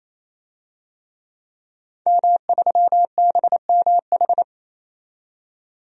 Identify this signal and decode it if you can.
Morse code, heard as a keyed tone, about 28 wpm, 700 Hz. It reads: M3BMH
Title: DXCC-100 Call Signs - 5 Characters - 28wpm